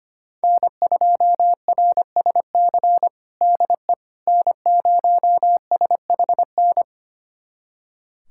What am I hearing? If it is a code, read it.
Morse code, heard as a keyed tone, about 25 wpm, 700 Hz. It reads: N2RSC DE N0SHN
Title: USA Callsign DE Callsign - 5 Characters - 25wpm